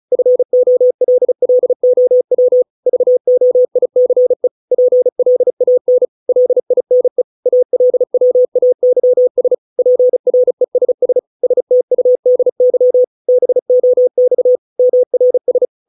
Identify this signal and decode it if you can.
Morse code, heard as a keyed tone, about 34 wpm, 500 Hz. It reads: FOLLOW VOICE PLAN LINE ALWAYS PRESS STUDY BOX MRS